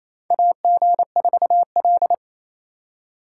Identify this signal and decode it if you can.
Morse code, heard as a keyed tone, about 28 wpm, 700 Hz. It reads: AG4L
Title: USA Call Signs - 3x Character Spacing - 28wpm